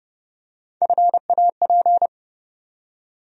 Morse code, heard as a keyed tone, about 30 wpm, 700 Hz. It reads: FAP
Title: USA Call Sign Postfixes - 3 Letters 30wpm